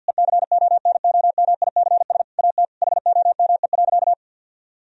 Morse code, heard as a keyed tone, about 50 wpm, 700 Hz. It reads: ECONOMICS AT HOME.
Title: Sentences from Top 1000 Words - Review Part 3 - 50wpm